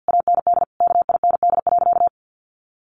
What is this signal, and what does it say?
Morse code, heard as a keyed tone, about 50 wpm, 700 Hz. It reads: AND KIND.
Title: Sentences from Top 1500 Words 50wpm